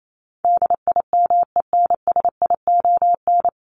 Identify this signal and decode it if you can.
Morse code, heard as a keyed tone, about 28 wpm, 700 Hz. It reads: DIMENSION
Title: Top 1501-1750 Words 28wpm